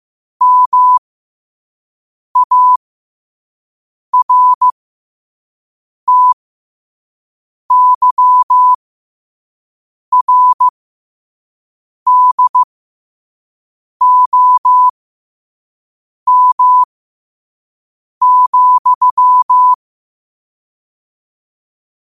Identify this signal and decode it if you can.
Morse code, heard as a keyed tone, about 15 wpm, spaced as for 5 wpm, 1000 Hz. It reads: MARTYRDOM,
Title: 5 WPM morse code quotes for Fri, 01 Mar 2024